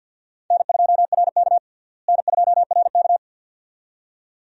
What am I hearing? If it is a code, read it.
Morse code, heard as a keyed tone, about 50 wpm, 700 Hz. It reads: N2RK N2RK
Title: USA Call Signs - 4 Characters - Repeated 2x - 50wpm